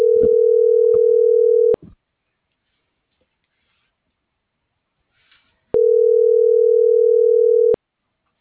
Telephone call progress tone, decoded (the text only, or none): ringback tone